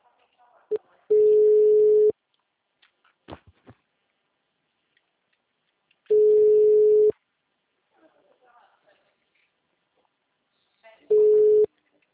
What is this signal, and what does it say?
Ringback tone (Europe, 425 Hz), with speech.